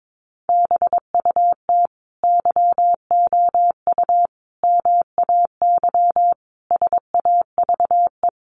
Morse code, heard as a keyed tone, about 22 wpm, 700 Hz. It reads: BUT YOU MAY SAVE
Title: Sentences from Top 1100 Words 22wpm